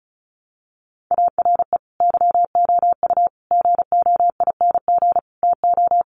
Morse code, heard as a keyed tone, about 34 wpm, 700 Hz. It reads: ARE YOU GOING TO
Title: Sets of 5 Words Encore 35wpm